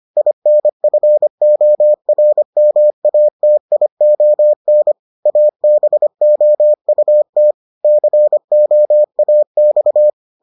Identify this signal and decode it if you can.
Morse code, heard as a keyed tone, about 25 wpm, 600 Hz. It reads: INFORMATION ABOUT COAX